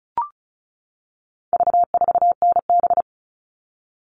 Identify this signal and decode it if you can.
Morse code, heard as a keyed tone, about 34 wpm, 700 Hz. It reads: V4NB
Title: All Cut Numbers - 4-Digit 35wpm